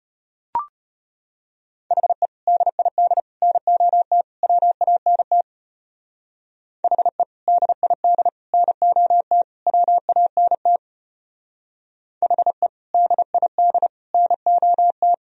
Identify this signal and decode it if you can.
Morse code, heard as a keyed tone, about 33 wpm, 700 Hz. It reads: HE DID NOT WANT HE DID NOT WANT HE DID NOT